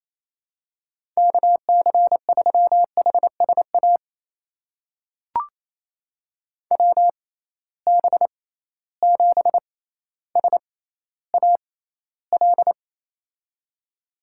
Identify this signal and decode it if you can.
Morse code, heard as a keyed tone, about 28 wpm, spaced as for 14 wpm, 700 Hz. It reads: KC3HSA WB7SAL
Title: USA Call Signs - 6 Characters - 6x Character Spacing - 28wpm